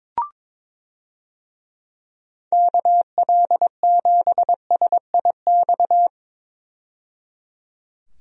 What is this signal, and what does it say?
Morse code, heard as a keyed tone, about 21 wpm, 700 Hz. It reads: KL7SIX
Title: DXCC-100 Call Signs - 6 Characters - 22wpm